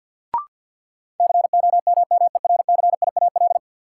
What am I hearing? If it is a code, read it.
Morse code, heard as a keyed tone, about 52 wpm, 700 Hz. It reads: COMMERCIAL
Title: Sentences from Top 1000 Words - Review Part 5 - New Word Drill - 50wpm